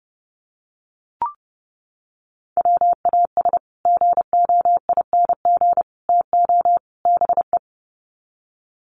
Morse code, heard as a keyed tone, about 30 wpm, 700 Hz. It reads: WAS GOING TO BE
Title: Sets of 4 Words 30wpm